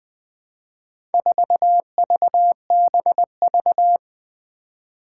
Morse code, heard as a keyed tone, about 20 wpm, 700 Hz. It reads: 4VBV
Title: Four Challenging Letter-Number Combinations 20wpm